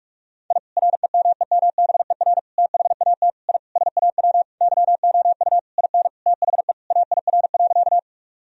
Morse code, heard as a keyed tone, about 46 wpm, 700 Hz. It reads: I REMEMBER THAT I SAW YOU IN THE AIR.
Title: Sentences from Top 1000 Words - Review Part 1 - 45wpm